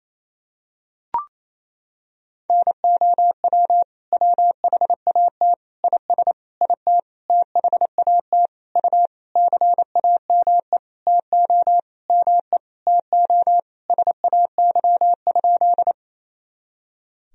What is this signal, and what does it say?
Morse code, heard as a keyed tone, about 28 wpm, 700 Hz. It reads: NOW WHAT IS IT THAT U CAME TO ME TO SAY?
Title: Sentences from Top 200 Words and Abbreviations 28wpm